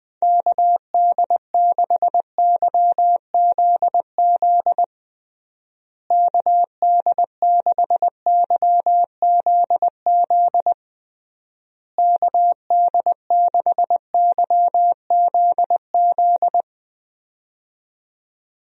Morse code, heard as a keyed tone, about 20 wpm, 700 Hz. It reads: KD6YZZ KD6YZZ KD6YZZ
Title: USA Call Signs - 6 Characters - Repeated 3x - 20wpm